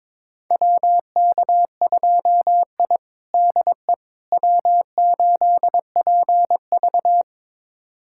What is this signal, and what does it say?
Morse code, heard as a keyed tone, about 22 wpm, 700 Hz. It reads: WK2I DE W8PV
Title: USA Callsign DE Callsign - 4 Characters - 22wpm